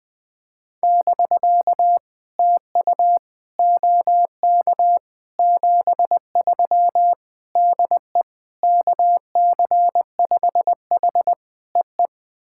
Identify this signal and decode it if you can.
Morse code, heard as a keyed tone, about 20 wpm, 700 Hz. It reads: <BK> TU OK 73 DE KC5H EE
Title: POTA QSO Practice - 20wpm